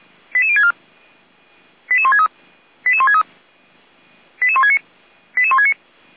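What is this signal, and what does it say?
Five-tone selective call (ZVEI), heard as quick runs of five tones, tones of about 70 ms. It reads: 88064 88163 88163 88169 88169
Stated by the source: ZVEI five-tone selective call received on an SDR.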